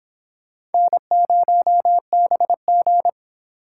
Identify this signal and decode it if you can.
Morse code, heard as a keyed tone, about 31 wpm, 700 Hz. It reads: N0BG